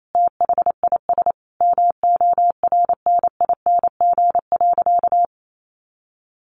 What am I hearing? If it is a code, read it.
Morse code, heard as a keyed tone, about 28 wpm, 700 Hz. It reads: THIS MORNING.
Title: Sentences from Top 900 Words 28wpm